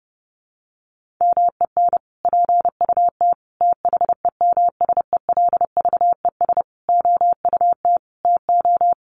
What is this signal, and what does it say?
Morse code, heard as a keyed tone, about 30 wpm, 700 Hz. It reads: MEN PUT THEMSELVES OUT TO
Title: Sentences from Top 1400 Words 30wpm